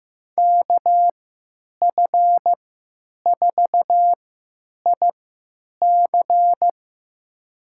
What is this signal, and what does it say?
Morse code, heard as a keyed tone, about 15 wpm, spaced as for 9 wpm, 700 Hz. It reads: KF4IC
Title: USA Call Signs - 5 Characters - 3x Character Spacing - 15wpm